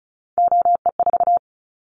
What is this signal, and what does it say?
Morse code, heard as a keyed tone, about 36 wpm, 700 Hz. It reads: OE4
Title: Callsign Prefixes for Top 25 DXCC - 35wpm